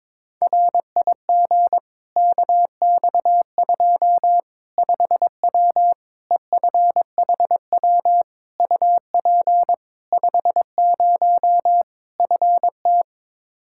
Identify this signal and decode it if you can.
Morse code, heard as a keyed tone, about 21 wpm, 700 Hz. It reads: RIG KX2 5W EFHW UP 50 FT
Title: Practice QSO - Exchange 2 - Info - 22wpm